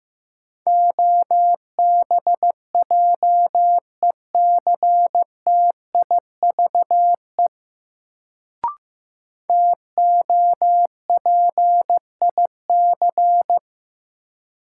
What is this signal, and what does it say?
Morse code, heard as a keyed tone, about 15 wpm, 700 Hz. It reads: OBJECTIVE TOPIC
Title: Sentences from Top 1200 Words - New Word Drill - 15wpm